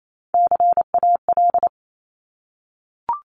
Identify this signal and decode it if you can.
Morse code, heard as a keyed tone, about 28 wpm, 700 Hz. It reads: CAL
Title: Top Three Letter Combinations in Words 28wpm